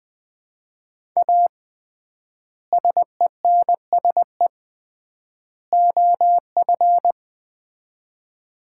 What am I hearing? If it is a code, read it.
Morse code, heard as a keyed tone, about 20 wpm, 700 Hz. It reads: A SENSE OF
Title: 3x Word-Spacing - Sets of 3 Words - 20wpm